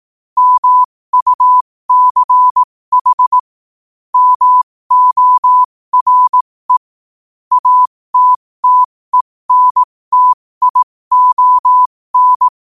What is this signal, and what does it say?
Morse code, heard as a keyed tone, about 18 wpm, spaced as for 15 wpm, 1000 Hz. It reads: MUCH MORE ATTENTION
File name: qotdfiles / 15wpm